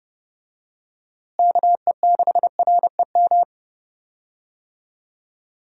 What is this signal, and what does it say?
Morse code, heard as a keyed tone, about 30 wpm, 700 Hz. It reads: KE6REM
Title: USA Call Signs - 6 Characters - Repeated 2x - 30wpm